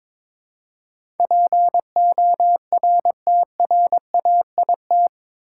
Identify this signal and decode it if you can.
Morse code, heard as a keyed tone, about 21 wpm, 700 Hz. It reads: PORTRAIT
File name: practice / 22wpm / Eight Letter Words - 22wpm